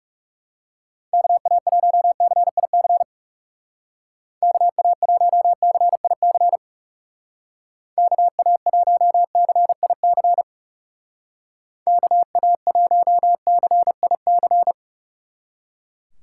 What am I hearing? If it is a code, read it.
Morse code, heard as a keyed tone, about 38 wpm, 700 Hz. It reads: KA1CIC KA1CIC KA1CIC KA1CIC